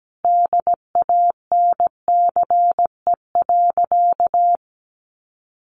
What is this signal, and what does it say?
Morse code, heard as a keyed tone, about 17 wpm, 700 Hz. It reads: DANCE.
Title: Sentences from Top 900 Words 17wpm